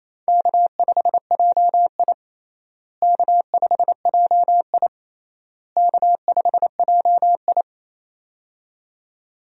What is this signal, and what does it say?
Morse code, heard as a keyed tone, about 28 wpm, 700 Hz. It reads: K5JI K5JI K5JI
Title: USA Call Signs - 4 Characters - Repeated 3x - 28wpm